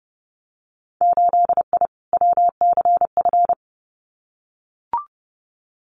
Morse code, heard as a keyed tone, about 30 wpm, 700 Hz. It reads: 8I WCF
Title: Winter Field Day Exchanges Encore 30wpm